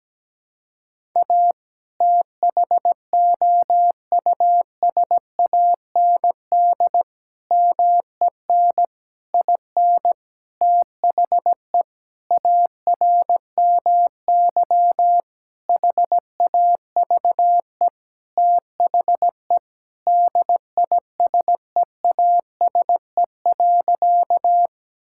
Morse code, heard as a keyed tone, about 17 wpm, 700 Hz. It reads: A THOUSAND MEN IN THE ARMY HAVE THE DISEASE.
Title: Sentences from Top 1000 Words - Review Part 6 - 17wpm